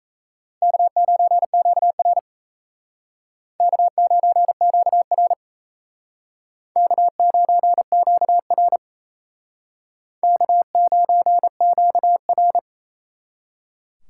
Morse code, heard as a keyed tone, about 35 wpm, 700 Hz. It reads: K9QR K9QR K9QR K9QR
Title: Speed Racing - USA 1x2 Call Signs 28wpm